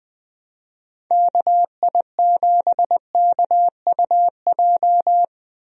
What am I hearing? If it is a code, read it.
Morse code, heard as a keyed tone, about 21 wpm, 700 Hz. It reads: KI7KUJ
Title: USA Call Signs - 6 Characters - Repeated 3x - 20wpm